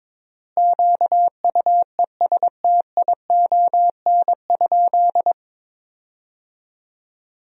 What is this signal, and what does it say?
Morse code, heard as a keyed tone, about 22 wpm, 700 Hz. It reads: QUESTION?
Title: 3x Word-Spacing - Sentences from Top 400 Words - 22wpm